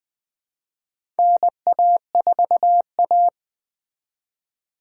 Morse code, heard as a keyed tone, about 21 wpm, 700 Hz. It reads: NA4A
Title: 2023 POTA Activator Call Signs - 20wpm